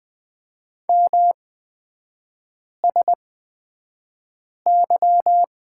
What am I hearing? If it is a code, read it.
Morse code, heard as a keyed tone, about 20 wpm, spaced as for 5 wpm, 700 Hz. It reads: MSY
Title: 250 Lesson 40e - BK - Three Characters 20wpm